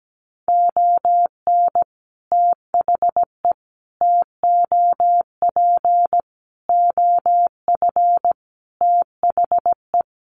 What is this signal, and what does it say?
Morse code, heard as a keyed tone, about 17 wpm, 700 Hz. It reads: ON THE TOP OF THE
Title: Sets of 5 Words Encore 17wpm